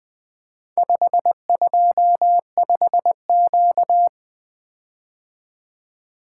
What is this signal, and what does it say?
Morse code, heard as a keyed tone, about 20 wpm, 700 Hz. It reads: 525Q
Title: Four Letter-Number Combinations - 20wpm